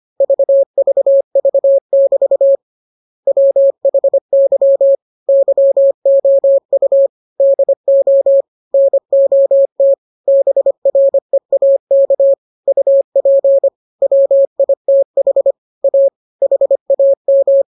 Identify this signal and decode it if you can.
Morse code, heard as a keyed tone, about 25 wpm, 550 Hz. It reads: VVV<BT> WHY YOU DO NOT BREAK UP WITH A HAM